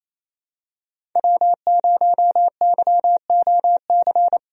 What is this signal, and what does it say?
Morse code, heard as a keyed tone, about 28 wpm, 700 Hz. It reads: W0YOC
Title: USA Call Signs - 5 Characters - 28wpm